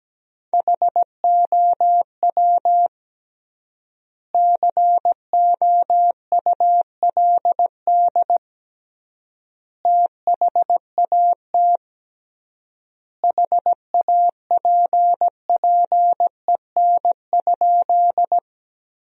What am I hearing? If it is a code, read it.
Morse code, heard as a keyed tone, about 17 wpm, 700 Hz. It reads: HOW COULD THAT HAPPEN?
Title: 3x Word-Spacing - Sentences from Top 400 Words - 17wpm